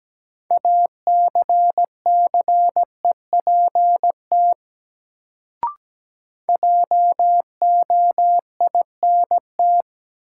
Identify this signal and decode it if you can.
Morse code, heard as a keyed tone, about 17 wpm, 700 Hz. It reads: ACCEPT JOINT
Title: Sentences from Top 1300 Words - New Word Drill - 17wpm